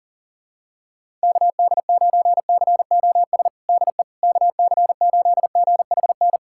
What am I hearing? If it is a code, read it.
Morse code, heard as a keyed tone, about 40 wpm, 700 Hz. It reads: KD9COS DE KC8GHN
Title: USA Callsign DE Callsign - 6 Characters - 40wpm